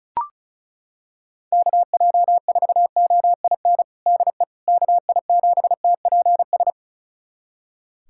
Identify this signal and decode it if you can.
Morse code, heard as a keyed tone, about 36 wpm, 700 Hz. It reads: KJ4OIN DE KI7TPS
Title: USA Callsign DE Callsign 35wpm